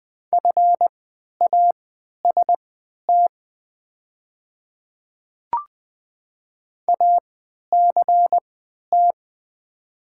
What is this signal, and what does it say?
Morse code, heard as a keyed tone, about 20 wpm, spaced as for 11 wpm, 700 Hz. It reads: FAST ACT
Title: 3x Character Spacing - Top 500 Words - 20wpm